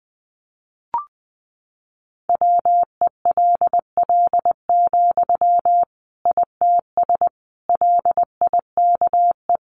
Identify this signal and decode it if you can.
Morse code, heard as a keyed tone, about 20 wpm, 700 Hz. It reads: WELL, ITS LIKE
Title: Sentences from Top 600 Words 20wpm